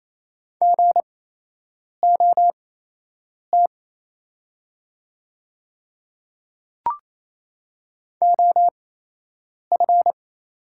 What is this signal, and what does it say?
Morse code, heard as a keyed tone, about 28 wpm, spaced as for 8 wpm, 700 Hz. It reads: GOT OF
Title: 8x Character Spacing - Top 300 Words - 28wpm